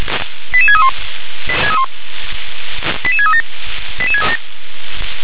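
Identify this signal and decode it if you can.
Five-tone selective call (ZVEI), heard as quick runs of five tones, tones of about 70 ms. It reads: 88631 88637
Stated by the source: ZVEI five-tone selective call received on an SDR.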